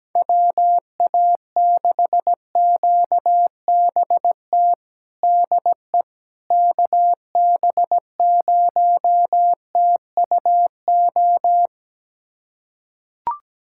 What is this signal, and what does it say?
Morse code, heard as a keyed tone, about 17 wpm, 700 Hz. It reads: WA6QBT DE KB0TUO
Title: USA Callsign DE Callsign 17wpm